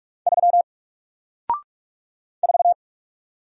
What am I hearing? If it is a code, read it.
Morse code, heard as a keyed tone, about 46 wpm, 700 Hz. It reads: 3 4
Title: Single Letter-Number 45wpm